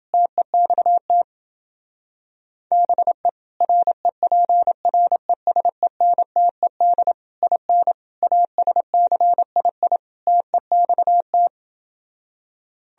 Morse code, heard as a keyed tone, about 28 wpm, 700 Hz. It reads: TEXT BE REPRESENTED IN ASCII TEXT